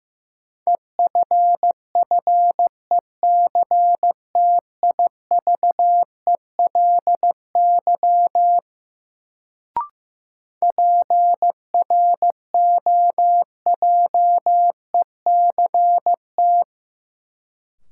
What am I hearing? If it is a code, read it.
Morse code, heard as a keyed tone, about 15 wpm, 700 Hz. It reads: EFFECTIVELY PROJECT
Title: Sentences from Top 1400 Words - New Word Drill Alt - 15wpm